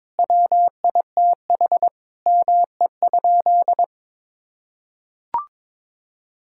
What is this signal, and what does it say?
Morse code, heard as a keyed tone, about 22 wpm, 700 Hz. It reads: WITH ME?
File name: Sentences with 7-Word Phrases - 22wpm